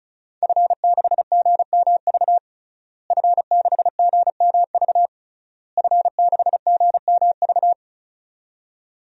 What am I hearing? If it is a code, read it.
Morse code, heard as a keyed tone, about 34 wpm, 700 Hz. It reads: F6GMV F6GMV F6GMV
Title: DXCC-100 Call Signs - 5 Characters - Repeated 3x - 35wpm